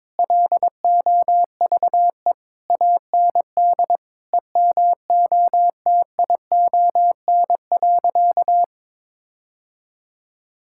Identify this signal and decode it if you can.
Morse code, heard as a keyed tone, about 22 wpm, 700 Hz. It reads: LOVE AND EMOTION.
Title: Sentences from Top 1600 Words 22wpm